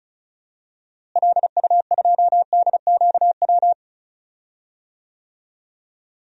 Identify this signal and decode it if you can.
Morse code, heard as a keyed tone, about 36 wpm, 700 Hz. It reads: LU2DQW
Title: DXCC-100 Call Signs - 6 Characters - Repeated 3x - 35wpm